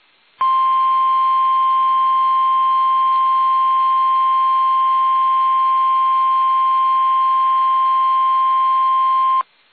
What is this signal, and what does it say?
Emergency Alert System (SAME) tones (the 1050 Hz attention tone). It is heard from 0.4 to 9.4 s.
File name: eas / KUSO / KUSO